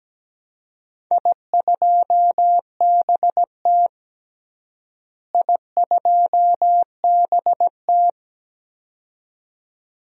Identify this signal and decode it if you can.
Morse code, heard as a keyed tone, about 17 wpm, 700 Hz. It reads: I2BT I2BT
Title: DXCC-100 Call Signs - 4 Characters - Repeated 2x - 17wpm